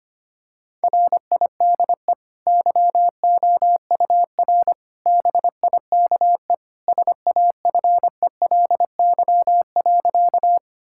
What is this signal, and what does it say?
Morse code, heard as a keyed tone, about 25 wpm, 690 Hz. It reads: RIDE YOUR BIKE SAFELY.
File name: Easy Sentences with Hints - 25wpm